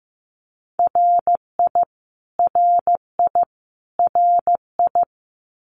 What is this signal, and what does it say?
Morse code, heard as a keyed tone, about 15 wpm, 700 Hz. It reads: RI RI RI
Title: US States Repeated 3x 15wpm